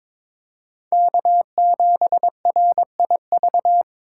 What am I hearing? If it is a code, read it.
Morse code, heard as a keyed tone, about 22 wpm, 700 Hz. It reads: K7RIV
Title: USA Call Signs - 5 Characters - 3x Character Spacing - 22wpm